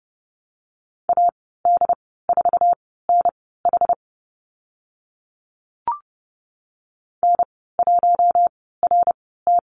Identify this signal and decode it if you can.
Morse code, heard as a keyed tone, about 30 wpm, spaced as for 17 wpm, 700 Hz. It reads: AD4NH N1RT
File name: Lesson 15g - Number 5 - Call Signs 30wpm